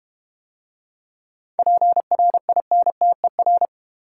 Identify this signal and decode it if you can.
Morse code, heard as a keyed tone, about 30 wpm, 700 Hz. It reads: PRINTER
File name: Easy Words - Speed-Racing with Hints - 25wpm